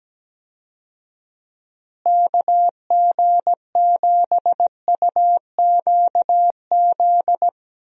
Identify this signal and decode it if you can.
Morse code, heard as a keyed tone, about 17 wpm, 700 Hz. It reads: KG7UQZ
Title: USA Call Signs - 6 Characters - Repeated 2x - 17wpm